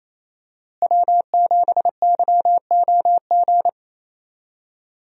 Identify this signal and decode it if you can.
Morse code, heard as a keyed tone, about 28 wpm, 700 Hz. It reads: W7YOG
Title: USA Call Signs - 5 Characters - Repeated 3x - 28wpm